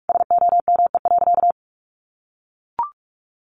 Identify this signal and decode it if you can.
Morse code, heard as a keyed tone, about 44 wpm, 700 Hz. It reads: SOME.
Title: Sentences from Top 800 Words 45wpm